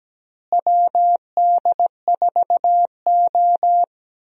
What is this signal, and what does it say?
Morse code, heard as a keyed tone, about 17 wpm, 700 Hz. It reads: WD4O
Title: USA Call Signs - 4 Characters - Repeated 3x - 17wpm